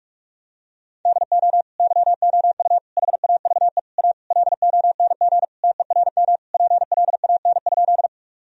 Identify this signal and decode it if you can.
Morse code, heard as a keyed tone, about 43 wpm, 700 Hz. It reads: DO YOU HAVE A LONG TERM PLAN?
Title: Sentences from Top 1000 Words - Review Part 4 - 45wpm